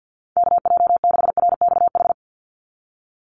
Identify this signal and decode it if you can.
Morse code, heard as a keyed tone, about 50 wpm, 700 Hz. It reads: KJ6RXH
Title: USA Call Signs 50wpm